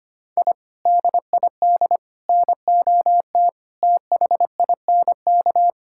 Morse code, heard as a keyed tone, about 25 wpm, 700 Hz. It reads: I DID NOT THINK
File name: practice / 25wpm / Sentences with 4-Word Phrases - 25wpm